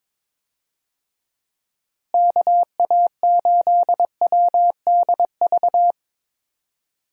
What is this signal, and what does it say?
Morse code, heard as a keyed tone, about 22 wpm, 700 Hz. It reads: KA8WDV
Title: USA Call Signs - 6 Characters - Repeated 3x - 22wpm